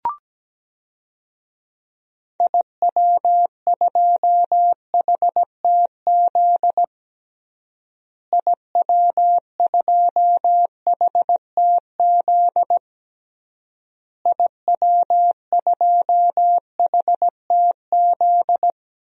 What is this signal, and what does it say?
Morse code, heard as a keyed tone, about 17 wpm, 700 Hz. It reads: IW2HTZ IW2HTZ IW2HTZ
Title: DXCC-100 Call Signs - 6 Characters - Repeated 3x - 17wpm